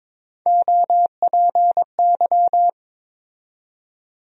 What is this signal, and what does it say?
Morse code, heard as a keyed tone, about 22 wpm, 700 Hz. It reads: OPY
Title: USA Call Sign Postfixes - 3 Letters 22wpm